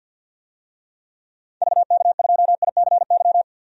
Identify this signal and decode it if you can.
Morse code, heard as a keyed tone, about 46 wpm, 700 Hz. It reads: VK2ICY
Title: DXCC-100 Call Signs - 6 Characters - 50wpm